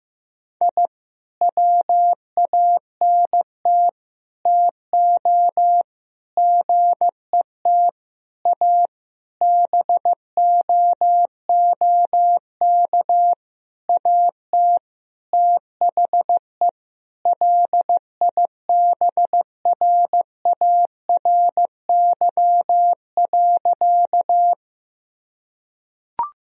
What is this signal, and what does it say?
Morse code, heard as a keyed tone, about 15 wpm, 700 Hz. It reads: I WANT TO GET A BOOK AT THE LIBRARY.
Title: Sentences from Top 1000 Words - Review Part 1 - 15wpm